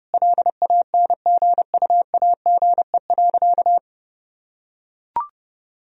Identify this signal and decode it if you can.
Morse code, heard as a keyed tone, about 32 wpm, 700 Hz. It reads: LANGUAGE.
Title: Sentences from Top 1000 Words - Review Part 3 - 30wpm